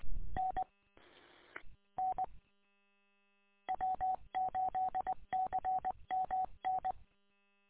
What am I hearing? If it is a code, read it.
Morse code, heard as a keyed tone, about 20 wpm, 750 Hz. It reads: N N W8CMN
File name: w8cmn- -combined